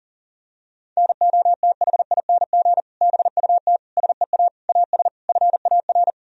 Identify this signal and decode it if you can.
Morse code, heard as a keyed tone, about 37 wpm, 700 Hz. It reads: NOTHING BUT SEA AS FAR
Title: Sentences from Top 1000 Words - Review Part 1 - 40wpm